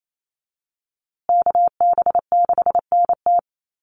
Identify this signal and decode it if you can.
Morse code, heard as a keyed tone, about 28 wpm, 700 Hz. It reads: KB6NT
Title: USA Call Signs Repeated 3x 28wpm